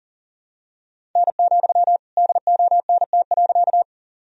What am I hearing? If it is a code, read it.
Morse code, heard as a keyed tone, about 37 wpm, 700 Hz. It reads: N, DONT.
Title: Sentences from Top 300 Words and Abbreviations 40wpm